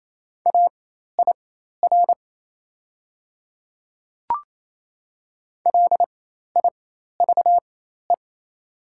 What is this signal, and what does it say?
Morse code, heard as a keyed tone, about 27 wpm, spaced as for 13 wpm, 700 Hz. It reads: AIR LIVE
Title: 4x Character Spacing - Top 200 Words - 28wpm